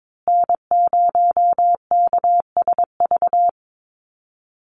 Morse code, heard as a keyed tone, about 22 wpm, 700 Hz. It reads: N0KSV
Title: USA Call Signs Repeated 2x 22wpm